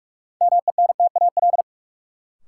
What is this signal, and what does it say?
Morse code, heard as a keyed tone, about 46 wpm, 700 Hz. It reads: MENTAL
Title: Sentences from Top 1000 Words - Review Part 4 - New Word Drill - 45wpm